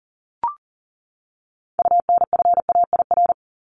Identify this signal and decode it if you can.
Morse code, heard as a keyed tone, about 40 wpm, 700 Hz. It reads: UNFAIR
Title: Top 1751-2000 Words 40wpm